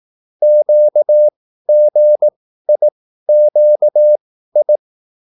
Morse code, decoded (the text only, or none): QGIQI